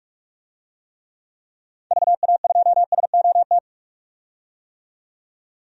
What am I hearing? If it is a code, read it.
Morse code, heard as a keyed tone, about 42 wpm, 700 Hz. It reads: VA2SOT
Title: DXCC-100 Call Signs - 6 Characters - 45wpm